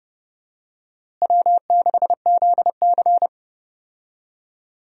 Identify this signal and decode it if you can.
Morse code, heard as a keyed tone, about 30 wpm, 700 Hz. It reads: W6ZC
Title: USA Call Signs - 4 Characters - Repeated 2x - 30wpm